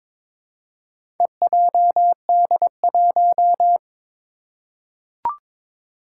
Morse code, heard as a keyed tone, about 22 wpm, 700 Hz. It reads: EJD1
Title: Four Challenging Letter-Number Combinations 22wpm